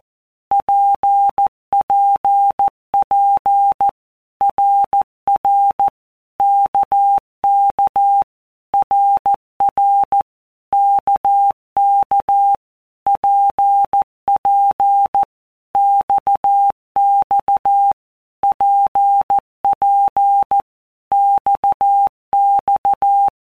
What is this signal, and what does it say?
Morse code, heard as a keyed tone, about 14 wpm, 800 Hz. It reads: PPP RR KK RR KK PP XX PP XX
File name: All Groups 1-minute Blocks / Groups@13 / RKPX